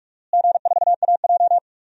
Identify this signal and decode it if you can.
Morse code, heard as a keyed tone, about 46 wpm, 700 Hz. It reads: G4AJ
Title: DXCC-100 Call Signs - 4 Characters - Repeated 2x - 45wpm